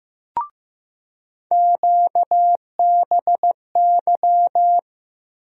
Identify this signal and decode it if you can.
Morse code, heard as a keyed tone, about 15 wpm, 700 Hz. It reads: QBY
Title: USA Call Sign Postfixes - 3 Letters 15wpm